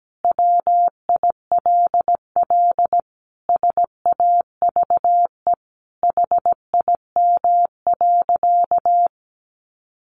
Morse code, heard as a keyed tone, about 17 wpm, 700 Hz. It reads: WILL SAVE HIM.
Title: Sentences from Top 1100 Words 17wpm